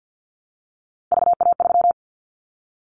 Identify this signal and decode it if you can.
Morse code, heard as a keyed tone, about 50 wpm, 700 Hz. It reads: VA3
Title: Callsign Prefixes for Top 25 DXCC - 50wpm